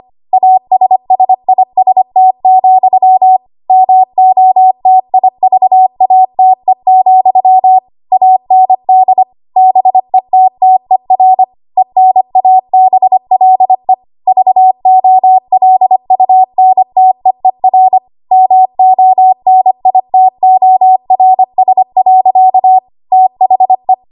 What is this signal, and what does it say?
Morse code, heard as a keyed tone, about 25 wpm, 750 Hz. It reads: ASSIST, MOTIVATE, AND BETTER ENABLE VOLUNTEER MONITORS. THE
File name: Archive / 25 WPM / 25WPM